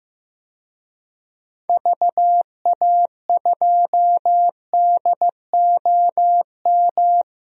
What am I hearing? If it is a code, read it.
Morse code, heard as a keyed tone, about 15 wpm, 700 Hz. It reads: VA2DOM
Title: DXCC-100 Call Signs - 6 Characters - Repeated 3x - 15wpm